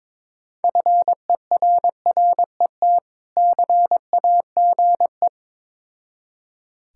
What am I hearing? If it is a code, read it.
Morse code, heard as a keyed tone, about 22 wpm, 690 Hz. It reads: FERRET CAGE
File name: Difficult Words with Hints - 22wpm